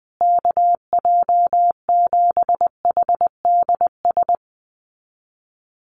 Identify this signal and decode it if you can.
Morse code, heard as a keyed tone, about 20 wpm, 700 Hz. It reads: KJ7HDS
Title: USA Call Signs Repeated 3x 20wpm